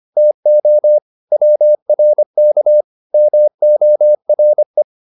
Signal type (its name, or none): Morse code